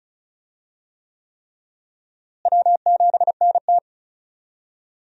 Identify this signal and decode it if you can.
Morse code, heard as a keyed tone, about 36 wpm, 700 Hz. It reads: W7NT
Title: USA Call Signs - 4 Characters - Repeated 3x - 35wpm